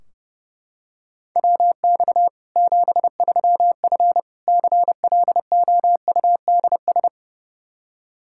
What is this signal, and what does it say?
Morse code, heard as a keyed tone, about 30 wpm, 700 Hz. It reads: WX 73F CLOUDS
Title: Practice QSO - Exchange 2 - Info - 30wpm